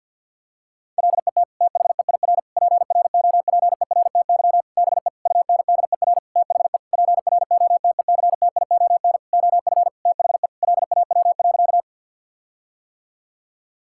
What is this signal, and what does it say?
Morse code, heard as a keyed tone, about 51 wpm, 700 Hz. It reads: LET THEIR PROPERTY BE UNDER THE PROTECTION OF THE LAW.
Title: Sentences from Top 1600 Words 50wpm